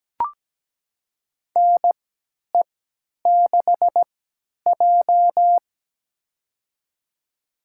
Morse code, heard as a keyed tone, about 17 wpm, spaced as for 10 wpm, 700 Hz. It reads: NE6J
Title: USA Call Signs - 3x Character Spacing - 17wpm